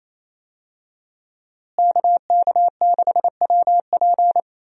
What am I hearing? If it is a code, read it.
Morse code, heard as a keyed tone, about 28 wpm, 700 Hz. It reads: KK6WP
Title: USA Call Signs - 5 Characters - Repeated 3x - 28wpm